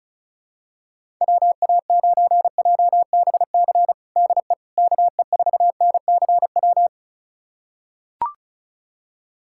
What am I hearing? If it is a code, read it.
Morse code, heard as a keyed tone, about 36 wpm, 700 Hz. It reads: WA9JBC DE KE4NCW
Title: USA Callsign DE Callsign - 6 Characters - 35wpm